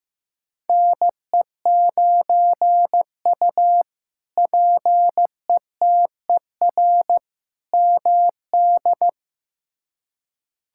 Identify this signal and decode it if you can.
Morse code, heard as a keyed tone, about 15 wpm, 700 Hz. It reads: NE9U PETER MD
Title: K1USN SST QSO Practice - 15wpm